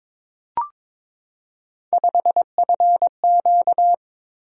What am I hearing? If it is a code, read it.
Morse code, heard as a keyed tone, about 22 wpm, 700 Hz. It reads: 5FQ
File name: practice / 22wpm / Three Letter-Number Combinations - 22wpm